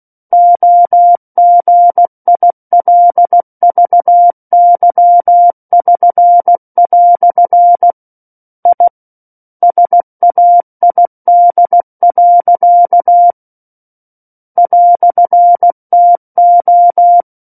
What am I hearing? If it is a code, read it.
Morse code, heard as a keyed tone, about 16 wpm, 700 Hz. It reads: OGILVY<SN>" I SAID. "TO
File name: Book One / 07-Chapter 7 - 16 WPM